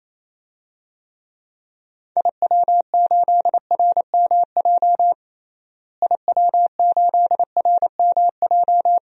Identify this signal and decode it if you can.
Morse code, heard as a keyed tone, about 28 wpm, 700 Hz. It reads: IW8RMJ IW8RMJ
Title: DXCC-100 Call Signs - 6 Characters - Repeated 2x - 28wpm